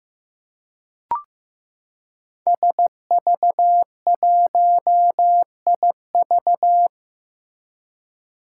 Morse code, heard as a keyed tone, about 15 wpm, 700 Hz. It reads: SV1IV
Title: Five Challenging Letter-Number Combinations 15wpm